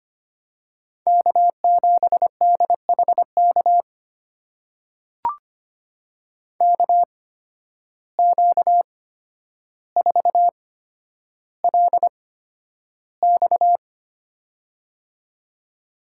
Morse code, heard as a keyed tone, about 25 wpm, spaced as for 11 wpm, 700 Hz. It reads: K7DHK KQ4LX
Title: USA Call Signs - 5 Characters - 8x Character Spacing - 25wpm